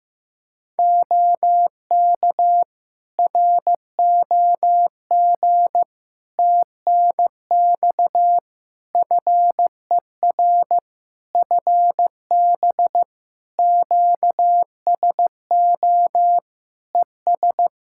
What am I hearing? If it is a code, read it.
Morse code, heard as a keyed tone, about 15 wpm, 700 Hz. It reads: OK ROG TNX FER FB QSO ES
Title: Practice QSO - Exchange 4 - Ending - 15wpm